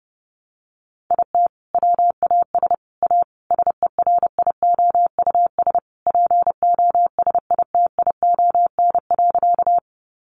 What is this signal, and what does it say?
Morse code, heard as a keyed tone, about 30 wpm, 700 Hz. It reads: IT WAS A SERIOUS POSITION.
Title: Sentences from Top 1100 Words 30wpm